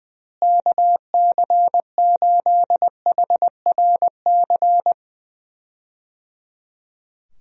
Morse code, heard as a keyed tone, about 21 wpm, 700 Hz. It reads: KC8HRC
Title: USA Call Signs - 6 Characters - Repeated 3x - 20wpm